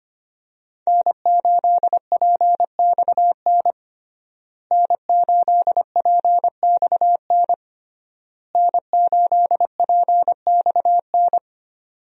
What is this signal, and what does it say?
Morse code, heard as a keyed tone, about 25 wpm, 700 Hz. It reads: N8PXN N8PXN N8PXN
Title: USA Call Signs - 5 Characters - Repeated 3x - 25wpm